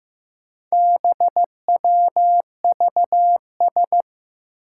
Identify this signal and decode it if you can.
Morse code, heard as a keyed tone, about 15 wpm, 700 Hz. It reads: BWVS
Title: Four Challenging Letter-Number Combinations 15wpm